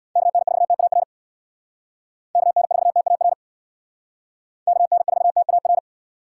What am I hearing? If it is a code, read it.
Morse code, heard as a keyed tone, about 63 wpm, 700 Hz. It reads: KN4TAR KN4TAR KN4TAR